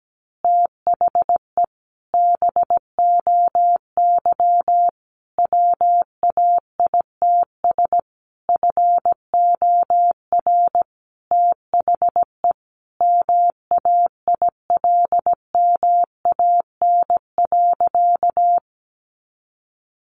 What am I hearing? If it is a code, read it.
Morse code, heard as a keyed tone, about 17 wpm, 700 Hz. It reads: THE BOY WAITS FOR THE MAILMAN.
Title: Easy Phrases 17wpm